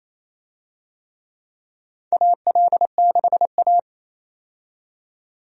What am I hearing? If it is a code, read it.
Morse code, heard as a keyed tone, about 28 wpm, 700 Hz. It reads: AL6A
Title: USA Call Signs - 4 Characters - 28wpm